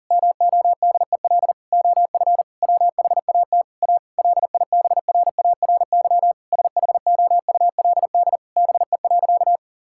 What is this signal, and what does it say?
Morse code, heard as a keyed tone, about 40 wpm, 700 Hz. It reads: MODEL OF WHAT A LIBRARY SHOULD BE.
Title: Sentences from Top 1000 Words - Review Part 5 - 40wpm